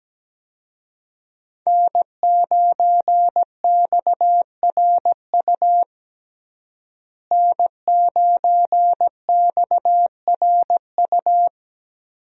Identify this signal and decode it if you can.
Morse code, heard as a keyed tone, about 17 wpm, 700 Hz. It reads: N9XRU N9XRU
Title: USA Call Signs - 5 Characters - Repeated 3x - 17wpm